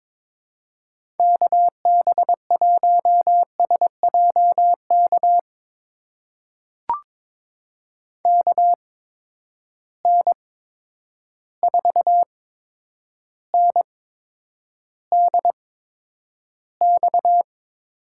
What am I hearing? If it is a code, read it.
Morse code, heard as a keyed tone, about 22 wpm, spaced as for 9 wpm, 700 Hz. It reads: KB1SJK KN4NDX
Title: USA Call Signs - 6 Characters - 8x Character Spacing - 22wpm